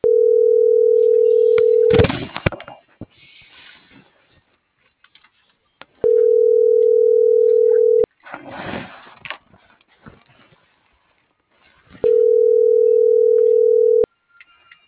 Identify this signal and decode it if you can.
Ringback tone (North America), 440 + 480 Hz.